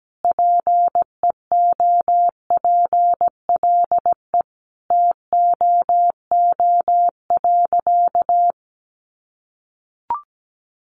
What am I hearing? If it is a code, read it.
Morse code, heard as a keyed tone, about 17 wpm, 700 Hz. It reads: PEOPLE TOO.
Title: Sentences from Top 1000 Words 17wpm